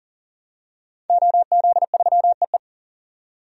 Morse code, heard as a keyed tone, about 40 wpm, 700 Hz. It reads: OZ3EE
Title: DXCC-100 Call Signs - 5 Characters - 40wpm